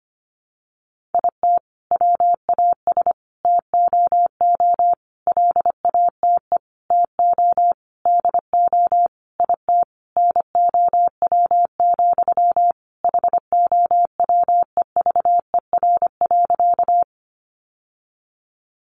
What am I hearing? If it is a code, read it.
Morse code, heard as a keyed tone, about 25 wpm, 700 Hz. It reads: IT WAS TOO LATE TO DO IT NOW, HOWEVER.
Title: Sentences from Top 600 Words 25wpm